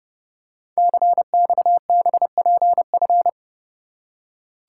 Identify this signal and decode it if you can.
Morse code, heard as a keyed tone, about 30 wpm, 700 Hz. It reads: CXBPF
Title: Code Groups - 5 Letters - 30wpm